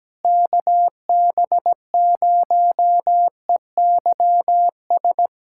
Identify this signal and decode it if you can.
Morse code, heard as a keyed tone, about 17 wpm, 700 Hz. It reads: KB0EYS
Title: USA Callsign DE Callsign - 6 Characters - 17wpm